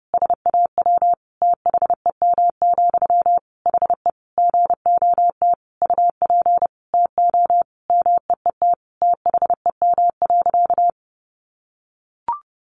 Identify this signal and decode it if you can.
Morse code, heard as a keyed tone, about 30 wpm, 700 Hz. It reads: SAW THEM, HE GOT UP TO MEET THEM.
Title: Sentences from Top 1100 Words 30wpm